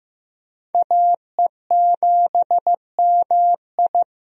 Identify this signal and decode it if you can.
Morse code, heard as a keyed tone, about 15 wpm, 700 Hz. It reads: AE7MI
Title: USA Call Signs - 5 Characters - 15wpm